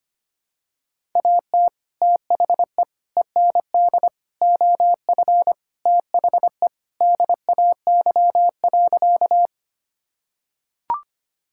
Morse code, heard as a keyed tone, about 25 wpm, 700 Hz. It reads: AT THE END OF THE DAY.
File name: practice / 25wpm / Sentences with 2-Word Phrases - 25wpm